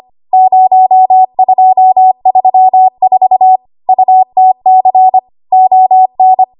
Morse code, heard as a keyed tone, about 25 wpm, 750 Hz. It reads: 0234 UTC ON
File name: Morse / 25WPM